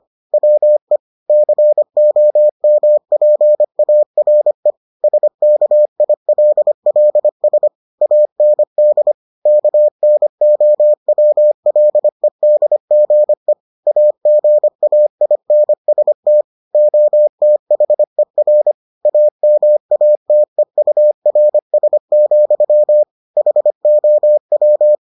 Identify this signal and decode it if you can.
Morse code, heard as a keyed tone, about 25 wpm, 600 Hz. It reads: WE COMPARE SKILLS AND KNOWLEDGE AGAINST OTHER AMATEURS, HOW